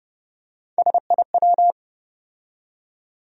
Morse code, heard as a keyed tone, about 30 wpm, 700 Hz. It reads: SIW
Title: USA Call Sign Postfixes - 3 Letters 30wpm